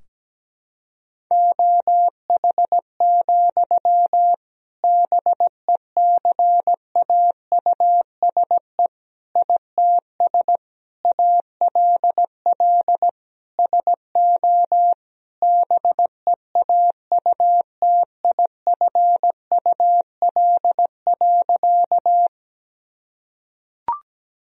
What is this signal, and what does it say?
Morse code, heard as a keyed tone, about 17 wpm, 700 Hz. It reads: OH, BECAUSE ITS ALL SO BEAUTIFUL.
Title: Sentences from Top 1000 Words - Review Part 3 - 17wpm